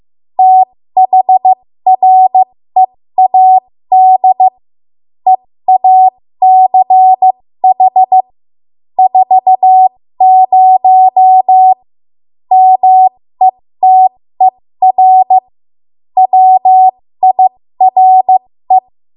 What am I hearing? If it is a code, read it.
Morse code, heard as a keyed tone, about 15 wpm, spaced as for 13 wpm, 750 Hz. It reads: THREAD EACH 40 METER WIRE
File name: Morse / 13WPM